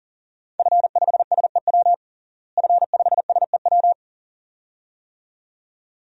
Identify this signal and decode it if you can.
Morse code, heard as a keyed tone, about 40 wpm, 690 Hz. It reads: F5SEW F5SEW
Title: DXCC-100 Call Signs - 5 Characters - Repeated 2x - 40wpm